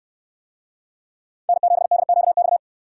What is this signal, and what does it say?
Morse code, heard as a keyed tone, about 65 wpm, 700 Hz. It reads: N7DQX